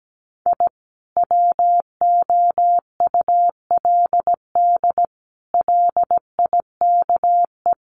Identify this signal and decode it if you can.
Morse code, heard as a keyed tone, about 17 wpm, 700 Hz. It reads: I WOULD LIKE
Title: Sets of 5 Words Encore2 17wpm